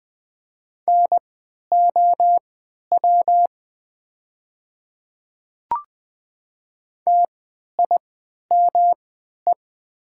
Morse code, heard as a keyed tone, about 20 wpm, spaced as for 11 wpm, 700 Hz. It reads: NOW TIME
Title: 3x Character Spacing - Top 100 Words - 20wpm